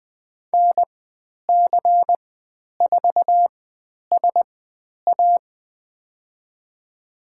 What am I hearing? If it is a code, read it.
Morse code, heard as a keyed tone, about 20 wpm, spaced as for 10 wpm, 700 Hz. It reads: NC4SA
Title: 109 Lesson 18g - Letter m - Call Signs 20wpm